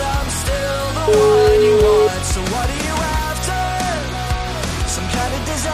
Ringback tone (Europe, 425 Hz), with noise.